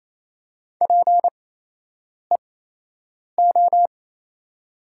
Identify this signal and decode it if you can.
Morse code, heard as a keyed tone, about 28 wpm, spaced as for 8 wpm, 700 Hz. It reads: PEO
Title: 8x Character Spacing - Top 100 Words - 28wpm